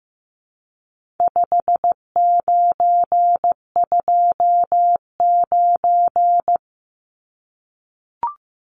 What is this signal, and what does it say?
Morse code, heard as a keyed tone, about 15 wpm, 700 Hz. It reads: 5929
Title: Numbers - 4-Digit 15wpm